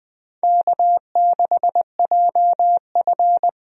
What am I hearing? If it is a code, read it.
Morse code, heard as a keyed tone, about 20 wpm, 700 Hz. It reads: K6JF
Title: USA Call Signs - 4 Characters - 20wpm